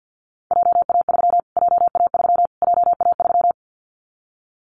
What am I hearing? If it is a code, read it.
Morse code, heard as a keyed tone, about 50 wpm, 700 Hz. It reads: JA3 JA3 JA3
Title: Callsign Prefixes for Top 25 DXCC - Repeated 3x - 50wpm